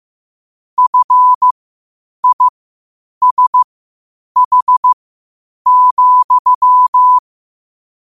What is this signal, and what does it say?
Morse code, heard as a keyed tone, about 15 wpm, spaced as for 8 wpm, 1000 Hz. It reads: FISH,
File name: qotdfiles / 8wpm